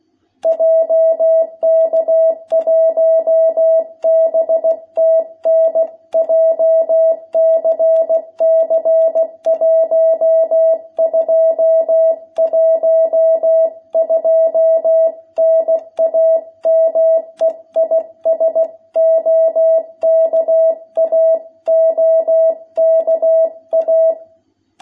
Morse code, heard as a keyed tone, about 16 wpm, 660 Hz. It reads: JK1BTN JCC 1212 NAME IS OKA OKA
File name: jh8cbh / 2021cw